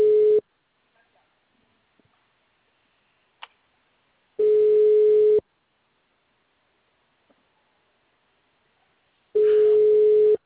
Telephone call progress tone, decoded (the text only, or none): ringback tone